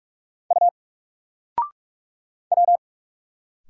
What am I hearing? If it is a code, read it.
Morse code, heard as a keyed tone, about 46 wpm, 700 Hz. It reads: U W
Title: Single Character (Letter:Number:Punctuation:Prosign) 45wpm